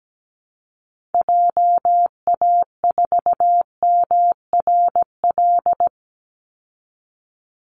Morse code, heard as a keyed tone, about 17 wpm, 700 Hz. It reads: JA4MRL
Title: Call Signs Repeated 3x - 17wpm